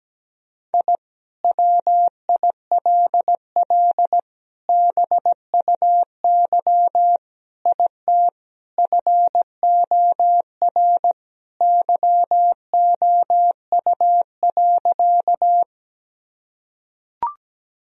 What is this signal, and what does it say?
Morse code, heard as a keyed tone, about 17 wpm, 700 Hz. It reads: I WILL BUY IT FOR YOU.
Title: Sentences from Top 1000 Words - Review Part 4 - 17wpm